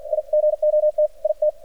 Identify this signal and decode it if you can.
Morse code, heard as a keyed tone, about 53 wpm, 600 Hz, with radio noise. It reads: I GOT IT
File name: archived / CRSnet